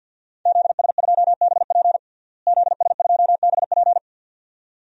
Morse code, heard as a keyed tone, about 53 wpm, 700 Hz. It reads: ZS2BP ZS2BP
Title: DXCC-100 Call Signs - 5 Characters - Repeated 2x - 50wpm